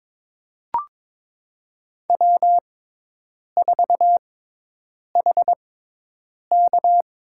Morse code, heard as a keyed tone, about 21 wpm, spaced as for 8 wpm, 700 Hz. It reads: W4HK
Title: USA Call Signs - 4 Characters - 6x Character Spacing - 22wpm